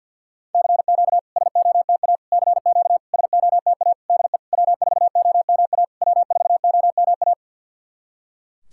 Morse code, heard as a keyed tone, about 48 wpm, 690 Hz. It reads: CQ SOTA CQ SOTA DE W4OMA W4OMA
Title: SOTA QSO Practice 50wpm